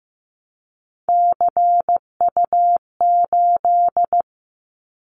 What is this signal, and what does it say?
Morse code, heard as a keyed tone, about 15 wpm, 700 Hz. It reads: CU8
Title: Callsign Prefixes for Top 100 DXCC - 15wpm